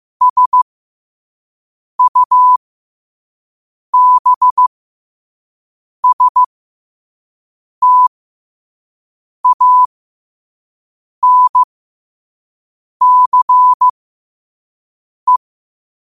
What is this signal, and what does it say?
Morse code, heard as a keyed tone, about 15 wpm, spaced as for 5 wpm, 1000 Hz. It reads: SUBSTANCE
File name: qotdfiles / 5wpm